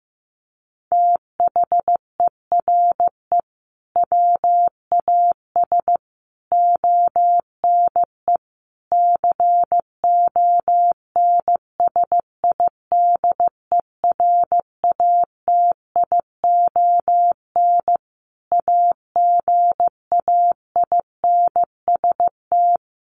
Morse code, heard as a keyed tone, about 15 wpm, 700 Hz. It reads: THERE WAS ONE CONSIDERATION AGAINST
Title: Sentences from Top 1500 Words 15wpm